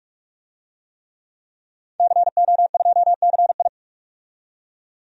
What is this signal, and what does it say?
Morse code, heard as a keyed tone, about 47 wpm, 700 Hz. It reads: CO2CI
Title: DXCC-100 Call Signs - 5 Characters - 45wpm